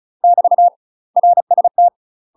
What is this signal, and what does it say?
Morse code, heard as a keyed tone, about 36 wpm, 700 Hz. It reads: <BT> RST
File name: UK MASTER QSOs TRAINING / 35WPM / UK MASTER QSOs 0040